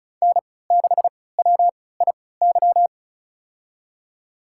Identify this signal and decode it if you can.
Morse code, heard as a keyed tone, about 34 wpm, spaced as for 20 wpm, 700 Hz. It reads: N6WIY
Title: USA Call Signs - 5 Characters - 3x Character Spacing - 35wpm